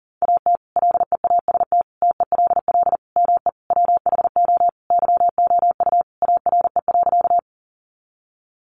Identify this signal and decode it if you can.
Morse code, heard as a keyed tone, about 40 wpm, 700 Hz. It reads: AT LEAST TELL ME WHO YOU ARE.
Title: Sentences from Top 700 Words 40wpm